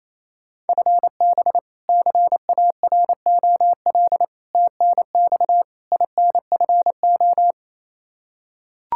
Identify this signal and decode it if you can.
Morse code, heard as a keyed tone, about 27 wpm, 700 Hz. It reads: FB CAROL TNX INFO
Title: Practice QSO - Exchange 3 - More Info - 28wpm